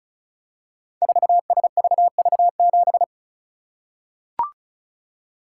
Morse code, heard as a keyed tone, about 36 wpm, 700 Hz. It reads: 4SVV7
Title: Code Groups - 5 Letter:Numbers - 35wpm